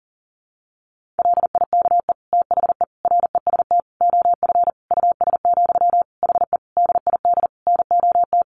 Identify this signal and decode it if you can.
Morse code, heard as a keyed tone, about 40 wpm, 700 Hz. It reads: LIKE THE REST OF US, HE DID NOT
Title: Sentences from Top 1400 Words 40wpm